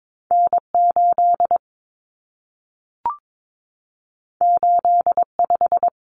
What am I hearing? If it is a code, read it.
Morse code, heard as a keyed tone, about 22 wpm, 700 Hz. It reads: N8 85
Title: Common Cut Numbers - 2-Digit 22wpm